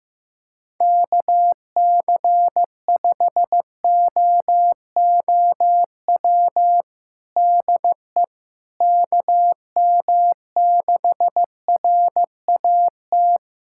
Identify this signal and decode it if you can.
Morse code, heard as a keyed tone, about 15 wpm, 700 Hz. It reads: KC5OOW DE KM6RAT
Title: USA Callsign DE Callsign - 6 Characters - 15wpm